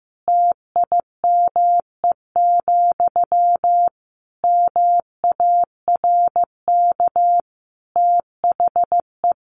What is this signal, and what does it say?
Morse code, heard as a keyed tone, about 15 wpm, 700 Hz. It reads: TIME, MARK THE
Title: Sentences from Top 1400 Words 15wpm